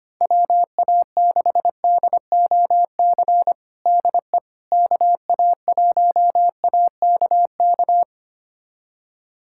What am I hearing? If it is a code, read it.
Morse code, heard as a keyed tone, about 25 wpm, 700 Hz. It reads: WA6DOC DE KA1AKK
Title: USA Callsign DE Callsign - 6 Characters - 25wpm